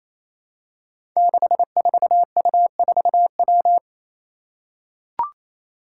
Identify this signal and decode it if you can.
Morse code, heard as a keyed tone, about 28 wpm, 700 Hz. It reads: 64U4W
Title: Five Challenging Letter-Number Combinations 28wpm